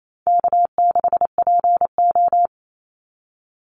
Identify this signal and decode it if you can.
Morse code, heard as a keyed tone, about 28 wpm, 700 Hz. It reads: K6PO
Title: Easy Call Signs Repeated 2x - 28wpm